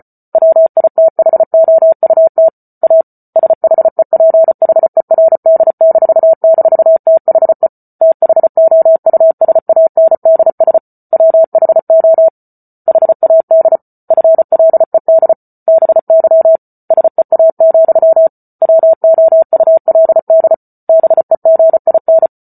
Morse code, heard as a keyed tone, about 34 wpm, 650 Hz. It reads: WITHOUT A SHEPHERD--THE THOUSANDS WHO HAD FLED BY SEA, WOULD BEGIN